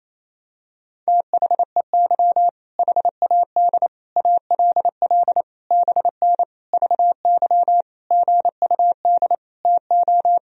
Morse code, heard as a keyed tone, about 27 wpm, 700 Hz. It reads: THEY HAD ALL BN VY GUD TO
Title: Sentences from Top 200 Words and Abbreviations 28wpm